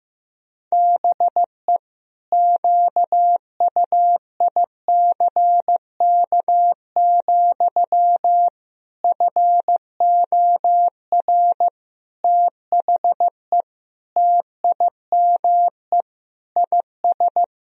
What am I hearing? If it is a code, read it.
Morse code, heard as a keyed tone, about 16 wpm, 700 Hz. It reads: BE QUICK, FOR THE TIME IS
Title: Sentences from Top 1000 Words - Review Part 2 - 15wpm